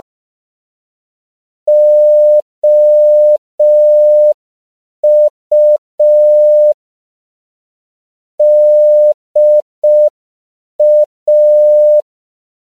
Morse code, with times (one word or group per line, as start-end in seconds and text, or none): 1.7-6.7 OU
8.4-12.0 DA